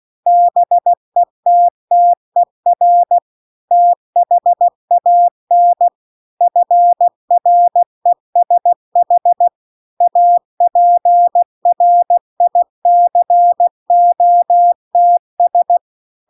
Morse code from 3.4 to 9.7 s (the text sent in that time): THAN FRESH